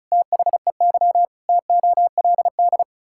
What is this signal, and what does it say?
Morse code, heard as a keyed tone, about 36 wpm, 700 Hz. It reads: THEY TOLD
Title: Pangram Challenge v2 - 35wpm